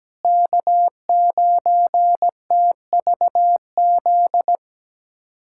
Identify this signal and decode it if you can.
Morse code, heard as a keyed tone, about 17 wpm, 700 Hz. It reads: K9TVZ
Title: USA Callsign DE Callsign - 5 Characters - 17wpm